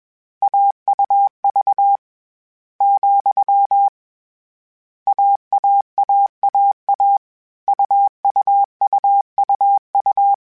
Morse code, heard as a keyed tone, about 21 wpm, 800 Hz. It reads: AUV , AAAAA UUUUU
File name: Difficult Combos / Difficult Combos-mp3s / AUV AUV4 @20